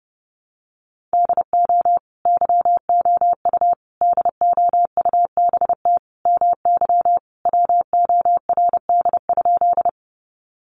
Morse code, heard as a keyed tone, about 30 wpm, 700 Hz. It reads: DO YOU DOUBT MY WORD?
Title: Sentences from Top 1500 Words 30wpm